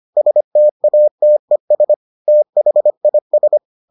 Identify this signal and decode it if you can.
Morse code, heard as a keyed tone, about 25 wpm, 600 Hz. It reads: STATES THIS